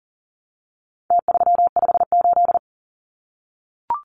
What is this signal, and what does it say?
Morse code, heard as a keyed tone, about 40 wpm, 700 Hz. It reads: T358
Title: Numbers - Contest Style - 4-Digit 40wpm